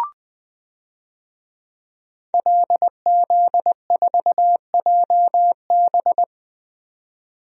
Morse code, heard as a keyed tone, about 20 wpm, 700 Hz. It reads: LZ4JB
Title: DXCC-100 Call Signs - 5 Characters - 20wpm